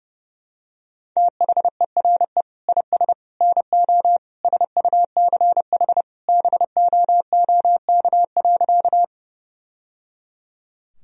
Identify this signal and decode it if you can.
Morse code, heard as a keyed tone, about 32 wpm, 700 Hz. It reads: THERE IS NO SUCH BOOK.
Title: Sentences from Top 1000 Words - Review Part 1 - 30wpm